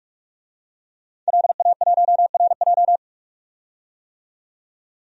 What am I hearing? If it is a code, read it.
Morse code, heard as a keyed tone, about 41 wpm, 700 Hz. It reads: LA1RJ
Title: DXCC-100 Call Signs - 5 Characters - 45wpm